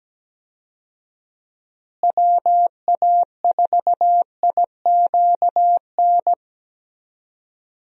Morse code, heard as a keyed tone, about 17 wpm, 700 Hz. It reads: WA4IQN
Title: USA Call Signs - 6 Characters - Repeated 3x - 17wpm